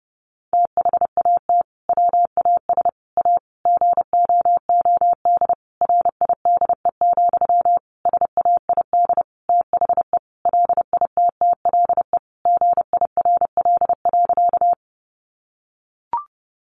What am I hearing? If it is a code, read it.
Morse code, heard as a keyed tone, about 30 wpm, 700 Hz. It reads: THAT WAS A GOOD RIDE, SAID THE LITTLE GIRL.
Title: Sentences from Top 1200 Words 30wpm